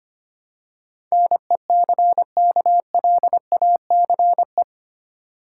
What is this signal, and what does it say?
Morse code, heard as a keyed tone, about 25 wpm, 700 Hz. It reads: NECKLACE
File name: practice / 25wpm / Easy Words with Hints - 25wpm